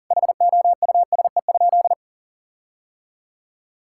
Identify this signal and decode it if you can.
Morse code, heard as a keyed tone, about 37 wpm, 700 Hz. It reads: HOUSE?
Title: Sentences from Top 1600 Words 40wpm